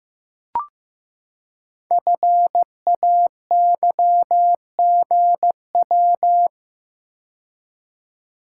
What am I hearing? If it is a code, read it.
Morse code, heard as a keyed tone, about 15 wpm, 700 Hz. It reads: FAYGW
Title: Code Groups - 5 Letters - 15wpm